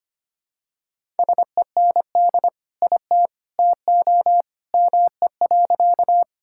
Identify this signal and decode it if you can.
Morse code, heard as a keyed tone, about 25 wpm, 700 Hz. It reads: SEND IT TO ME.
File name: practice / 25wpm / Sentences with 2-Word Phrases - 25wpm